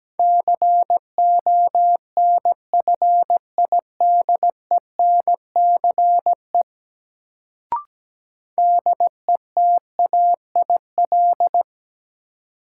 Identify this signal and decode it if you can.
Morse code, heard as a keyed tone, about 17 wpm, 700 Hz. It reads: CONFIDENCE DETAIL
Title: Sentences from Top 1500 Words - New Word Drill Alt - 17wpm